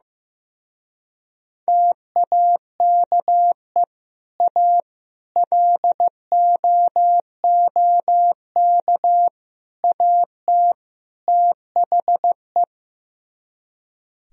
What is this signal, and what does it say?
Morse code, heard as a keyed tone, about 15 wpm, 700 Hz. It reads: TAKE A LOOK AT THE
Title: Speed-Racing - Sets of 5 Words 15wpm